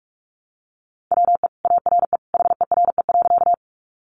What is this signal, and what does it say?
Morse code, heard as a keyed tone, about 44 wpm, 700 Hz. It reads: WE ARE HERE.
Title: Sentences from Top 200 Words 45wpm